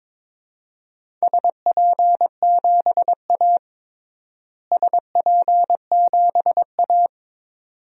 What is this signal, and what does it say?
Morse code, heard as a keyed tone, about 22 wpm, 700 Hz. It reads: SP7A SP7A
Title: DXCC-100 Call Signs - 4 Characters - Repeated 2x - 22wpm